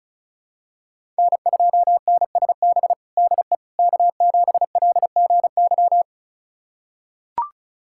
Morse code, heard as a keyed tone, about 36 wpm, 700 Hz. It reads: N2NSB DE K7LGY
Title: USA Callsign DE Callsign - 5 Characters - 35wpm